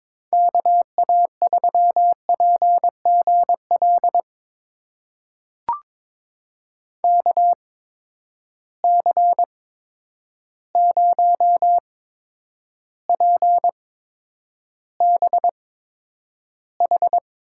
Morse code, heard as a keyed tone, about 22 wpm, spaced as for 9 wpm, 700 Hz. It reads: KA3PGL KC0PBH
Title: USA Call Signs - 6 Characters - 8x Character Spacing - 22wpm